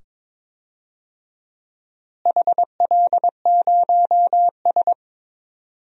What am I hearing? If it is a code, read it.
Morse code, heard as a keyed tone, about 21 wpm, 700 Hz. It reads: HL0S
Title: DXCC-100 Call Signs - 4 Characters - Repeated 2x - 22wpm